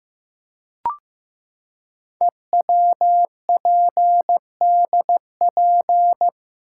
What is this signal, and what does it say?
Morse code, heard as a keyed tone, about 15 wpm, 700 Hz. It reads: EWPDP
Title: Code Groups - 5 Letters - 15wpm